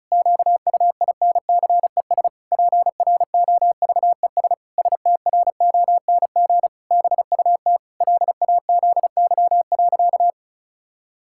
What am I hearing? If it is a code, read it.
Morse code, heard as a keyed tone, about 34 wpm, 700 Hz. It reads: QUINCES PROVES STRONG BUT LAZY.
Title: Pangram Challenge v2 - 35wpm